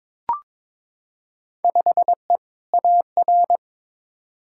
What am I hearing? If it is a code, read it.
Morse code, heard as a keyed tone, about 22 wpm, 700 Hz. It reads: 5E AR
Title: Field Day Exchanges Encore - 22wpm